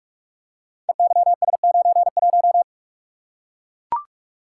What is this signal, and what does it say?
Morse code, heard as a keyed tone, about 44 wpm, 700 Hz. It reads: EYS91
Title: Code Groups - 5 Letter:Numbers - 45wpm